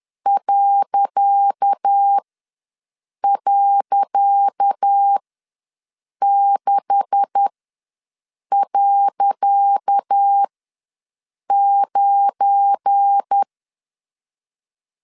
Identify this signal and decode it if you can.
Morse code, heard as a keyed tone, about 10 wpm, spaced as for 6 wpm, 800 Hz. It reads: ..6.9
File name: LEZIONI / 10 SIGNS / 10C DOT ws 10wpm 4213